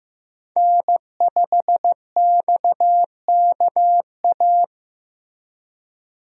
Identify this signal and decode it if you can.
Morse code, heard as a keyed tone, about 15 wpm, 700 Hz. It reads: N5XKA
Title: USA Call Signs - 5 Characters - Repeated 2x - 15wpm